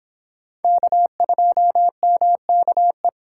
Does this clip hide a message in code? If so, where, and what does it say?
Yes: Morse code from 0.6 to 3.1 s, heard as a keyed tone, about 25 wpm, 700 Hz. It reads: K2MKE